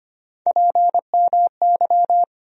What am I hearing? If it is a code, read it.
Morse code, heard as a keyed tone, about 25 wpm, 700 Hz. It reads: PMY
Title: USA Call Sign Postfixes - 3 Letters 25wpm